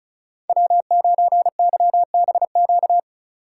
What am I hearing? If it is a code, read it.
Morse code, heard as a keyed tone, about 34 wpm, 700 Hz. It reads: W9YBQ
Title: USA Call Signs - 5 Characters - 3x Character Spacing - 35wpm